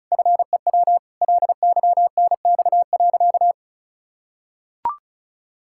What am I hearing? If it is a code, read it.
Morse code, heard as a keyed tone, about 36 wpm, 700 Hz. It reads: FEW LYNX.
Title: Pangram Challenge v2 - 35wpm